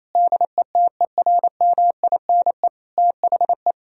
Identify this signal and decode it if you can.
Morse code, heard as a keyed tone, about 28 wpm, 700 Hz. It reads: DETERMINE THE
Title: Sentences from Top 1000 Words - Review Part 4 - 28wpm